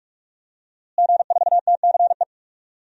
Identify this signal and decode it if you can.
Morse code, heard as a keyed tone, about 46 wpm, 700 Hz. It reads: G4TCE
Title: DXCC-100 Call Signs - 5 Characters - Repeated 3x - 45wpm